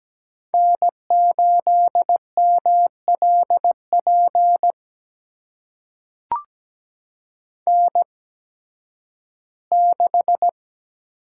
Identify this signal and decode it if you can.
Morse code, heard as a keyed tone, about 17 wpm, spaced as for 11 wpm, 700 Hz. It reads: N8MLP N6
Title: USA Call Signs - 5 Characters - 8x Character Spacing - 17wpm